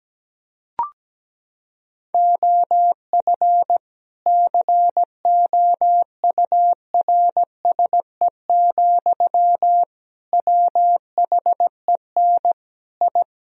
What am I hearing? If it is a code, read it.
Morse code, heard as a keyed tone, about 17 wpm, 700 Hz. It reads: OF COURSE, WHEN I
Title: Sentences from Top 1000 Words - Review Part 3 - 17wpm